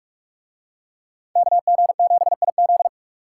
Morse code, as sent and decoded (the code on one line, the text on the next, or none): -.- --. --... .. --..
KG7IZ